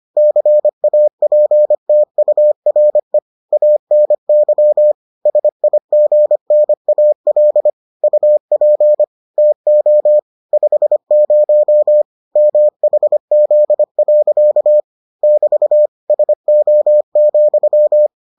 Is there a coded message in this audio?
Yes: Morse code from 0.2 to 18.1 s, heard as a keyed tone, about 25 wpm, 600 Hz. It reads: CAPTURE ANY SIGNAL UP TO 50 MHZ. <BT> SO,